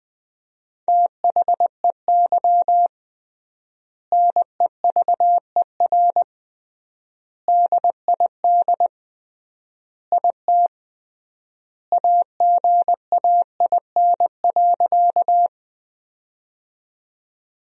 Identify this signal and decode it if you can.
Morse code, heard as a keyed tone, about 20 wpm, 700 Hz. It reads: THEY NEVER DID IT AGAIN.
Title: 3x Word-Spacing - Sentences from Top 300 Words - 20wpm